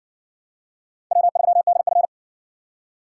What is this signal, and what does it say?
Morse code, heard as a keyed tone, about 58 wpm, 700 Hz. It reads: W3DF